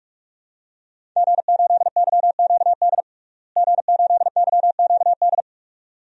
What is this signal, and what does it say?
Morse code, heard as a keyed tone, about 47 wpm, 700 Hz. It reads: G8YQD G8YQD
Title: DXCC-100 Call Signs - 5 Characters - Repeated 2x - 45wpm